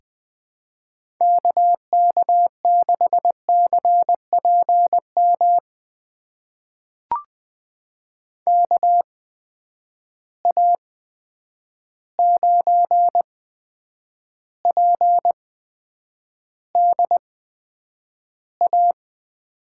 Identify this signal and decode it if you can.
Morse code, heard as a keyed tone, about 20 wpm, spaced as for 9 wpm, 700 Hz. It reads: KK6CPM KA9PDA
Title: USA Call Signs - 6 Characters - 8x Character Spacing - 20wpm